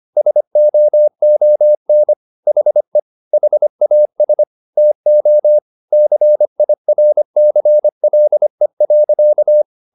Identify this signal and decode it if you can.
Morse code, heard as a keyed tone, about 25 wpm, 600 Hz. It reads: SOON HE HAS TO CIRCLE.